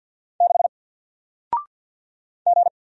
Morse code, heard as a keyed tone, about 50 wpm, 700 Hz. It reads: 6 G
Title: Single Letter-Number Alt 50wpm